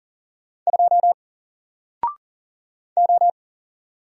Morse code, heard as a keyed tone, about 40 wpm, 700 Hz. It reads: 2 O
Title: Single Letter-Number 40wpm